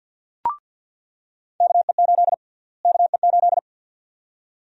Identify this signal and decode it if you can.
Morse code, heard as a keyed tone, about 48 wpm, 700 Hz. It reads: KE8 KE8
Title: USA Call Sign Prefixes - Repeated 2x - 50wpm